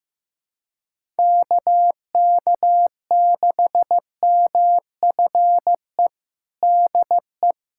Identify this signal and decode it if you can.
Morse code, heard as a keyed tone, about 16 wpm, 700 Hz. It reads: KK6MFE DE
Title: USA Callsign DE Callsign - 6 Characters - 15wpm